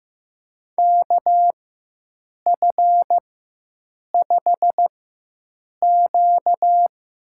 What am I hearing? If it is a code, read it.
Morse code, heard as a keyed tone, about 15 wpm, spaced as for 7 wpm, 700 Hz. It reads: KF5Q
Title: USA Call Signs - 4x Character Spacing - 15wpm